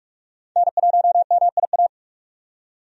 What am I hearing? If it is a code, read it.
Morse code, heard as a keyed tone, about 46 wpm, 700 Hz. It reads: N1MIA
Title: USA Call Signs - 5 Characters - 45wpm